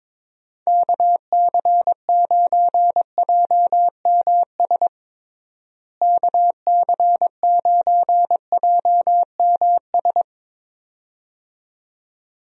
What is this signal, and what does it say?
Morse code, heard as a keyed tone, about 22 wpm, 700 Hz. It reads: KC9JMS KC9JMS
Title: USA Call Signs - 6 Characters - Repeated 2x - 22wpm